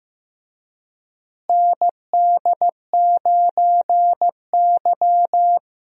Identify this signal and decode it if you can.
Morse code, heard as a keyed tone, about 15 wpm, 700 Hz. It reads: ND9Y
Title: USA Call Signs - 4 Characters - Repeated 3x - 15wpm